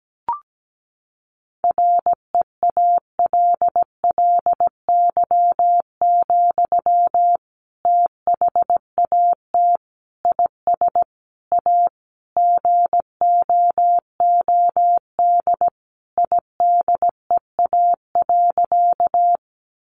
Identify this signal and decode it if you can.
Morse code, heard as a keyed tone, about 17 wpm, 700 Hz. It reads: REALLY, THAT IS A GOOD IDEA.
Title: Sentences from Top 600 Words 17wpm